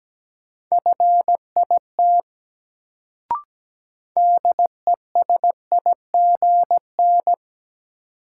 Morse code, heard as a keyed tone, about 17 wpm, 700 Hz. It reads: FIT DESIGN
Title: Sentences from Top 800 Words - New Word Drill - 17wpm